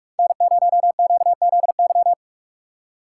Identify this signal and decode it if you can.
Morse code, heard as a keyed tone, about 44 wpm, 700 Hz. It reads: N0QZY
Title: USA Call Signs - 5 Characters - 3x Character Spacing - 45wpm